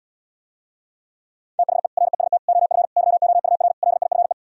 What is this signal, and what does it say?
Morse code, heard as a keyed tone, about 77 wpm, 700 Hz. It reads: THE REST OF YOUR LIFE